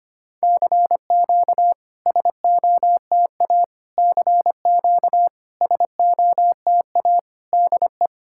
Morse code, heard as a keyed tone, about 25 wpm, 690 Hz. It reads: CQ SOTA CQ SOTA DE
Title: SOTA QSO Practice 25wpm